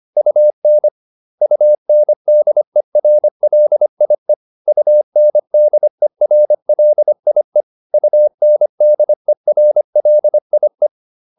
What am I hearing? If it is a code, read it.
Morse code, heard as a keyed tone, about 25 wpm, 600 Hz. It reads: UN UNDERLIE UNDERLIE UNDERLIE